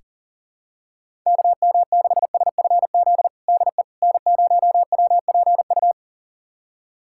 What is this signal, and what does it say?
Morse code, heard as a keyed tone, about 40 wpm, 700 Hz. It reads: KM6SFZ DE N0WPU
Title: USA Callsign DE Callsign 40wpm